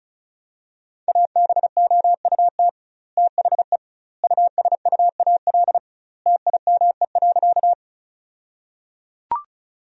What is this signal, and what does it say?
Morse code, heard as a keyed tone, about 34 wpm, 700 Hz. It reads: ABOUT THE USUAL TIME.
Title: 2x Word-Spacing - Sentences from Top 400 Words - 35wpm